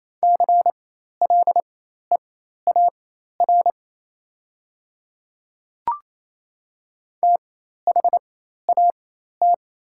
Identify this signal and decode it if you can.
Morse code, heard as a keyed tone, about 28 wpm, spaced as for 13 wpm, 690 Hz. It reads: CLEAR THAT
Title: 4x Character Spacing - Top 500 Words - 28wpm